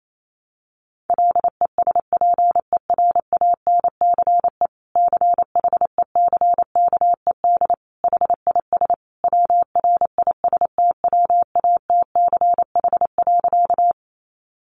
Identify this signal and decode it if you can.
Morse code, heard as a keyed tone, about 28 wpm, 700 Hz. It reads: LESPERANCE CHECKED HIS WRISTWATCH.
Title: A Sound Of Thunder 28wpm